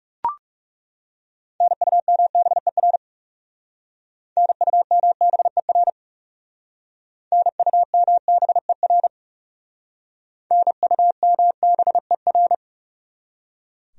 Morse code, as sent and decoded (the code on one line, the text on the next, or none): -. ..- -- -... .-. / -. ..- -- -... .-. / -. ..- -- -... .-. / -. ..- -- -... .-.
NUMBER NUMBER NUMBER NUMBER